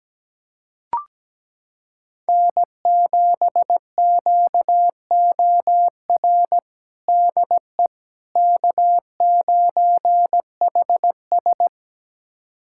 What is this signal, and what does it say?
Morse code, heard as a keyed tone, about 17 wpm, 700 Hz. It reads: N7QOR DE K9HS
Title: USA Callsign DE Callsign 17wpm